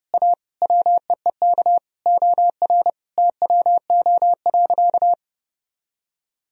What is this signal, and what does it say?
Morse code, heard as a keyed tone, about 30 wpm, 700 Hz. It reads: A WEEK OR TWO.
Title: Sentences from Top 1000 Words - Review Part 2 - 30wpm